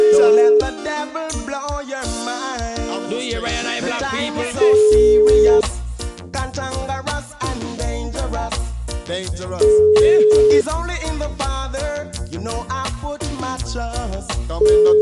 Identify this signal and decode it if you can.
Ringback tone (Europe, 425 Hz), with noise.